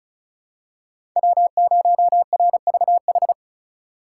Morse code, heard as a keyed tone, about 34 wpm, 700 Hz. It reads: W0RVH
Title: USA Call Signs - 2x Character Spacing - 35wpm